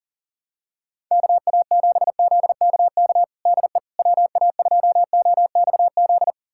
Morse code, heard as a keyed tone, about 40 wpm, 700 Hz. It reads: KA7ZKK DE WA2OXZ
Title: USA Callsign DE Callsign - 6 Characters - 40wpm